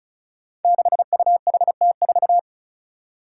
Morse code, heard as a keyed tone, about 36 wpm, 700 Hz. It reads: 6UHT4
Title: Five Challenging Letter-Number Combinations 35wpm